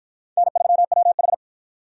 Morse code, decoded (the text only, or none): N3WH